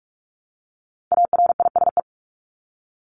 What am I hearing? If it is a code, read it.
Morse code, heard as a keyed tone, about 44 wpm, 700 Hz. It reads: ARISE
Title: Top 1501-1750 Words 45wpm